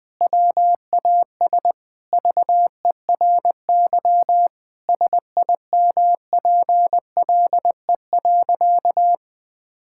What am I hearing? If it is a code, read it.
Morse code, heard as a keyed tone, about 20 wpm, 700 Hz. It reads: WAS VERY SIMPLE.
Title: Sentences from Top 1600 Words 20wpm